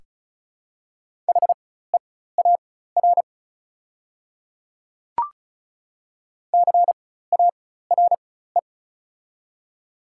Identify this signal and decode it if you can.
Morse code, heard as a keyed tone, about 36 wpm, spaced as for 16 wpm, 700 Hz. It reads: HEAR CARE
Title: 4x Character Spacing - Top 500 Words - 35wpm